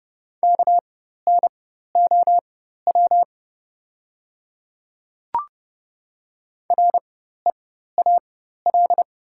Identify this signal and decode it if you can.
Morse code, heard as a keyed tone, about 29 wpm, spaced as for 14 wpm, 700 Hz. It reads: KNOW REAL
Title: 4x Character Spacing - Top 300 Words - 30wpm